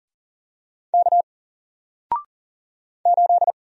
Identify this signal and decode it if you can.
Morse code, heard as a keyed tone, about 40 wpm, 700 Hz. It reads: K 8
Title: Single Character (Letter:Number:Punctuation:Prosign) Alt 40wpm